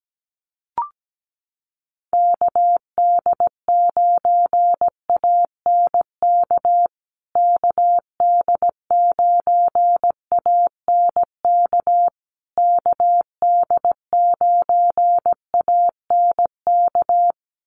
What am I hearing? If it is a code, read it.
Morse code, heard as a keyed tone, about 17 wpm, 700 Hz. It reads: KD9ANK KD9ANK KD9ANK
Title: USA Call Signs Repeated 3x 17wpm